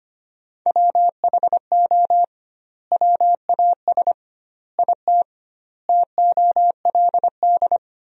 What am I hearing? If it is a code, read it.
Morse code, heard as a keyed tone, about 25 wpm, 700 Hz. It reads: WHO WAS IT TOLD
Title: 2x Word-Spacing - Sentences from Top 400 Words - 25wpm